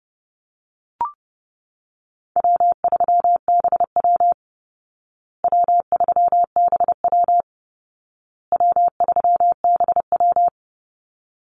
Morse code, heard as a keyed tone, about 28 wpm, 700 Hz. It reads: W3BW W3BW W3BW
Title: Easy Call Signs Repeated 3x - 30wpm